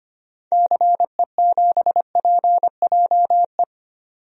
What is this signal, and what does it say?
Morse code, heard as a keyed tone, about 25 wpm, 700 Hz. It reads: CE7PJE
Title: DXCC-100 Call Signs - 6 Characters - 25wpm